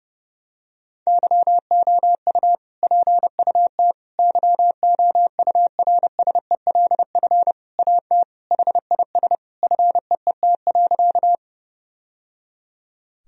Morse code, heard as a keyed tone, about 30 wpm, 700 Hz. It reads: YOU PUT YOURSELF AT HIS FEET.
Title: Sentences from Top 1000 Words - Review Part 1 - 30wpm